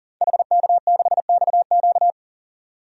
Morse code, heard as a keyed tone, about 40 wpm, 700 Hz. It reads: HK6XQ
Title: DXCC-100 Call Signs - 5 Characters - 40wpm